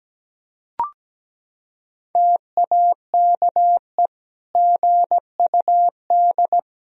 Morse code, heard as a keyed tone, about 17 wpm, 700 Hz. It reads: TAKE GUD
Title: Sentences from Top 500 Words and Abbreviations 17wpm